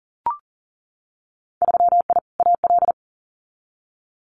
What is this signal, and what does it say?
Morse code, heard as a keyed tone, about 38 wpm, 700 Hz. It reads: 3I AL
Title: Winter Field Day Exchanges 40wpm